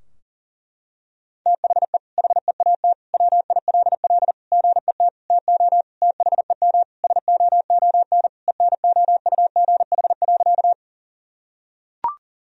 Morse code, heard as a keyed tone, about 37 wpm, 700 Hz. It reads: THE HEAT WILL GET TO THEM SOON ENOUGH.
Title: Sentences from Top 1000 Words - Review Part 3 - 40wpm